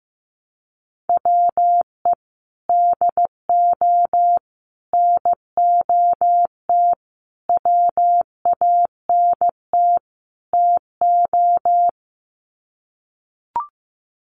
Morse code, heard as a keyed tone, about 15 wpm, 700 Hz. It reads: WE DO NOT WANT TO
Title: Sets of 5 Words 15wpm